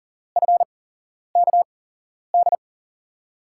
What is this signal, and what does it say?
Morse code, heard as a keyed tone, about 40 wpm, spaced as for 11 wpm, 700 Hz. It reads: FKD
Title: USA Call Signs - 6 Characters - 8x Character Spacing - 40wpm